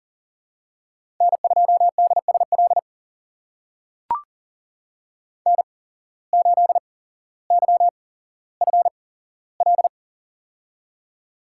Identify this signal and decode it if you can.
Morse code, heard as a keyed tone, about 38 wpm, spaced as for 17 wpm, 700 Hz. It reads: N2DSL N8YFL
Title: USA Call Signs - 8x Character Spacing - 40wpm